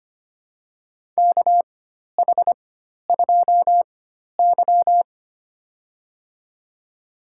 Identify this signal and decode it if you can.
Morse code, heard as a keyed tone, about 25 wpm, spaced as for 12 wpm, 700 Hz. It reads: KH2Y
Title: USA Call Signs - 4x Character Spacing - 25wpm